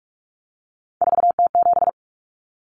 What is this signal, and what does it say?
Morse code, heard as a keyed tone, about 44 wpm, 700 Hz. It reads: 4T7
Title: Common Cut Numbers - 3-Digit 45wpm